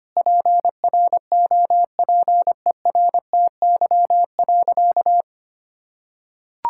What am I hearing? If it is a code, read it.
Morse code, heard as a keyed tone, about 25 wpm, 700 Hz. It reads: PROPERTY.
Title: Sentences from Top 1600 Words 25wpm